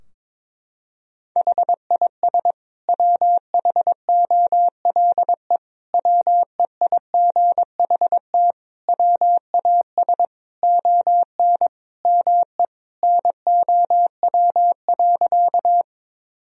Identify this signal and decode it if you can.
Morse code, heard as a keyed tone, about 22 wpm, 700 Hz. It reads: HIS WHOLE WEIGHT WAS ON ME NOW.
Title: Sentences from Top 1000 Words - Review Part 3 - 22wpm